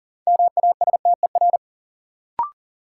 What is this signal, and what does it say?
Morse code, heard as a keyed tone, about 37 wpm, 700 Hz. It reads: MASTER
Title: Sentences from Top 1300 Words - New Word Drill - 40wpm